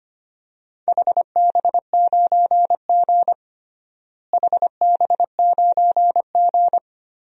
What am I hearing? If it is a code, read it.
Morse code, heard as a keyed tone, about 25 wpm, 700 Hz. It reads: HB9G HB9G
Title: DXCC-100 Call Signs - 4 Characters - Repeated 3x - 25wpm